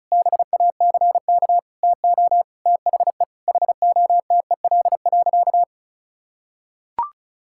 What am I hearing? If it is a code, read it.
Morse code, heard as a keyed tone, about 36 wpm, 700 Hz. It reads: BACK TO THE HOTEL.
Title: Sentences from Top 1600 Words 35wpm